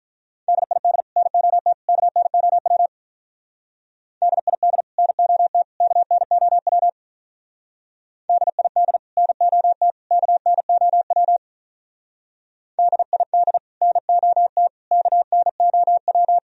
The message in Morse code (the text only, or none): DID NOT KNOW DID NOT KNOW DID NOT KNOW DID NOT KNOW